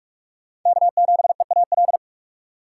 Morse code, heard as a keyed tone, about 42 wpm, 700 Hz. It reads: K7EAL